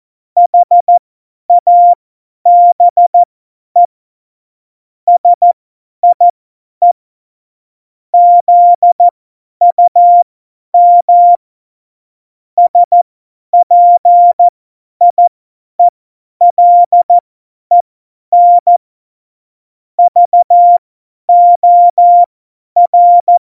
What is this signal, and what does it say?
Morse code, heard as a keyed tone, about 14 wpm, spaced as for 10 wpm, 700 Hz. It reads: HABE SIE ZUM SPIELEN VOR
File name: Morsekurs / Witze / Witz 05. Januar 2021, 50 F70